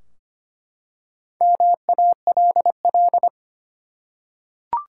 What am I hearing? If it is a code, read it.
Morse code, heard as a keyed tone, about 25 wpm, 700 Hz. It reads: MALL
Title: Four Letter Words - 25wpm